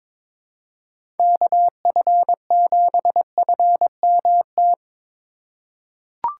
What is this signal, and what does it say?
Morse code, heard as a keyed tone, about 21 wpm, 700 Hz. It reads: KF7FMT
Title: USA Call Signs - 6 Characters - 2x Character Spacing - 22wpm